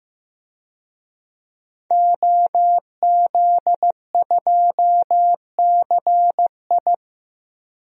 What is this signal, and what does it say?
Morse code, heard as a keyed tone, about 15 wpm, 700 Hz. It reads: OZ2CI
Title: DXCC-100 Call Signs - 5 Characters - 15wpm